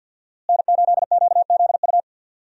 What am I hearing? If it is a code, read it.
Morse code, heard as a keyed tone, about 49 wpm, 700 Hz. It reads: N8QZU
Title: USA Call Signs - 5 Characters - Repeated 3x - 50wpm